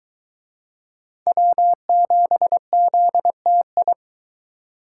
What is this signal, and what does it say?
Morse code, heard as a keyed tone, about 21 wpm, 700 Hz. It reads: W7ZTI